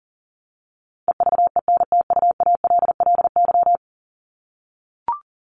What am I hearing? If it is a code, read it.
Morse code, heard as a keyed tone, about 40 wpm, 700 Hz. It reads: EVENTUALLY
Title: LY Suffix - Top 100 Words with Base Words 40wpm